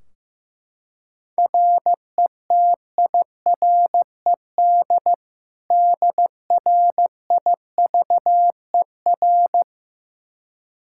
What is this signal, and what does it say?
Morse code, heard as a keyed tone, about 15 wpm, 700 Hz. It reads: RETIRED DRIVER
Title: Practice QSO - Exchange 3 - More Info - 15wpm